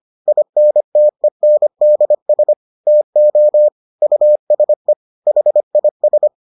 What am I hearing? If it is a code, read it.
Morse code, heard as a keyed tone, about 25 wpm, 600 Hz. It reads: INTENDS TO USE HIS